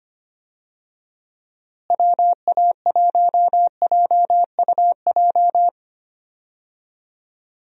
Morse code, heard as a keyed tone, about 25 wpm, 700 Hz. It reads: WA1JUJ
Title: USA Call Signs - 6 Characters - Repeated 3x - 25wpm